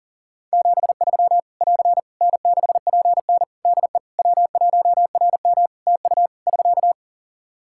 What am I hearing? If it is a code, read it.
Morse code, heard as a keyed tone, about 40 wpm, 700 Hz. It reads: 73 <AR> N6PN DE W1RM TU <SK>
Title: Practice QSO - Exchange 4 - Ending - 40wpm